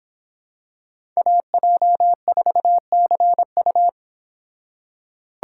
Morse code, heard as a keyed tone, about 24 wpm, 700 Hz. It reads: AJ4CU